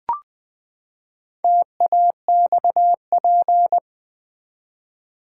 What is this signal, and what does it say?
Morse code, heard as a keyed tone, about 20 wpm, 700 Hz. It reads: TAXP
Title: Build A Word - 4 Letters At A Time - 8-Letter Words - 20wpm